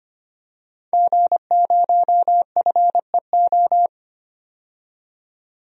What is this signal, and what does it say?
Morse code, heard as a keyed tone, about 25 wpm, 700 Hz. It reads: G0FEO
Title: DXCC-100 Call Signs - 5 Characters - 25wpm